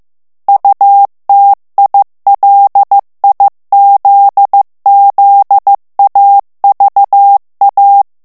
Morse code, heard as a keyed tone, about 15 wpm, 800 Hz. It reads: UTILIZZAVA
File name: ITALIEN texte clair 1 vitesse 14 mots minute